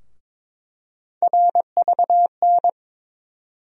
Morse code, heard as a keyed tone, about 21 wpm, 700 Hz. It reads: RVN
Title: USA Call Sign Postfixes - 3 Letters 22wpm